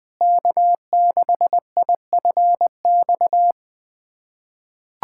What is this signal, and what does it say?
Morse code, heard as a keyed tone, about 20 wpm, 700 Hz. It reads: K6IFX
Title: USA Callsign DE Callsign - 5 Characters - 20wpm